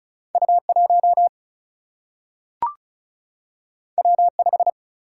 Morse code, heard as a keyed tone, about 36 wpm, 700 Hz. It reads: U 1 W 5
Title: Single-Letter Single-Number 35wpm